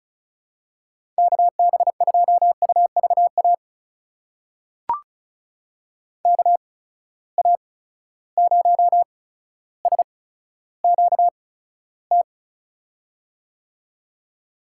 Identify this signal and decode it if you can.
Morse code, heard as a keyed tone, about 36 wpm, spaced as for 15 wpm, 700 Hz. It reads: KB2UVA KA0SQT
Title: USA Call Signs - 6 Characters - 8x Character Spacing - 35wpm